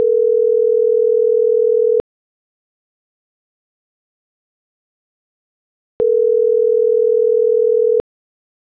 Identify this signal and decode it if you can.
Ringback tone (North America), 440 + 480 Hz.
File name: CENTRAL MAERSK / out kevin.riosmsk